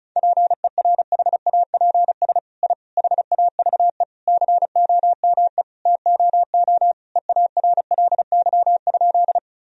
Morse code, heard as a keyed tone, about 34 wpm, 700 Hz. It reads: PERHAPS I HAVE COME TOO EARLY?
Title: Sentences from Top 1000 Words - Review Part 2 - 35wpm